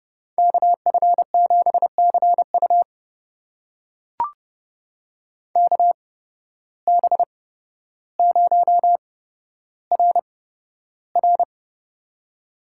Morse code, heard as a keyed tone, about 30 wpm, spaced as for 13 wpm, 700 Hz. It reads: KF7CU KB0RR
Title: USA Call Signs - 5 Characters - 8x Character Spacing - 30wpm